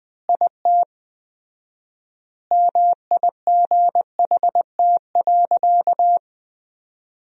Morse code, heard as a keyed tone, about 21 wpm, 700 Hz. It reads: IT MIGHT.
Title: 4x Word-Spacing - Sentences from Top 500 Words - 20wpm